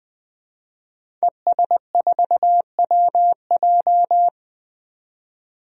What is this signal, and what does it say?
Morse code, heard as a keyed tone, about 20 wpm, 700 Hz. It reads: ES4WJ
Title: DXCC-100 Call Signs - 5 Characters - 20wpm